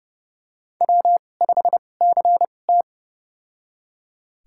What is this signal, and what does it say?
Morse code, heard as a keyed tone, about 32 wpm, spaced as for 22 wpm, 700 Hz. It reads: W5CT
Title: USA Call Signs - 4 Characters - 2x Character Spacing - 30wpm